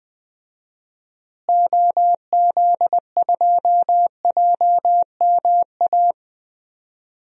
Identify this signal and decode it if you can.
Morse code, heard as a keyed tone, about 20 wpm, 700 Hz. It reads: OZ2JMA
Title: DXCC-100 Call Signs - 6 Characters - Repeated 3x - 20wpm